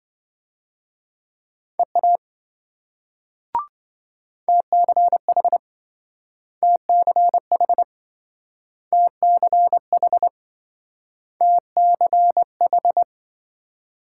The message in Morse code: EA TCH TCH TCH TCH

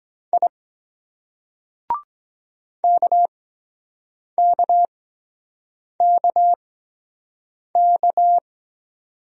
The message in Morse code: I KKKK

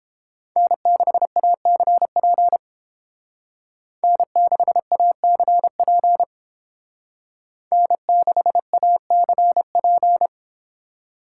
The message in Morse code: N6ACP N6ACP N6ACP

